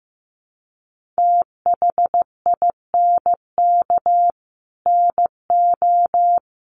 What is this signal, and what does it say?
Morse code, heard as a keyed tone, about 14 wpm, 700 Hz. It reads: THINK NO
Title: Sentences from Top 200 Words 15wpm